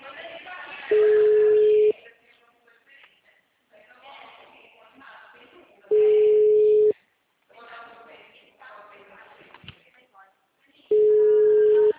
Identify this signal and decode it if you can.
Ringback tone (Europe, 425 Hz), with speech.
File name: out- -2003- - - .21593